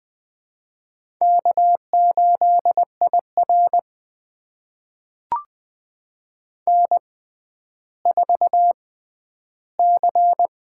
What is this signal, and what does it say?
Morse code, heard as a keyed tone, about 20 wpm, spaced as for 11 wpm, 700 Hz. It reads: K8IR N4C
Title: USA Call Signs - 4 Characters - 6x Character Spacing - 20wpm